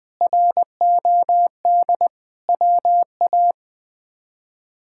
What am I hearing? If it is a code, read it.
Morse code, heard as a keyed tone, about 20 wpm, 700 Hz. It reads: ROD WA
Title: K1USN SST QSO Practice - 20wpm